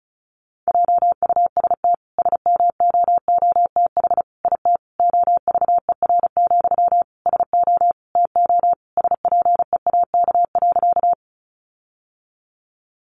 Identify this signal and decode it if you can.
Morse code, heard as a keyed tone, about 34 wpm, 700 Hz. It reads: JUST SMOOTH IT OVER, SO TO SPEAK.
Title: Sentences from Top 1400 Words 35wpm